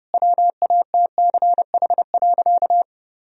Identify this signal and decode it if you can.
Morse code, heard as a keyed tone, about 30 wpm, 700 Hz. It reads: WATCH.
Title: Sentences from Top 1000 Words - Review Part 1 - 30wpm